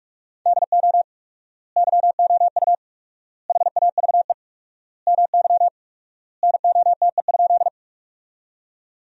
Morse code, heard as a keyed tone, about 46 wpm, 700 Hz. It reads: DO YOU HAVE MY NOTE?
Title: 4x Word-Spacing - Sentences from Top 500 Words - 45wpm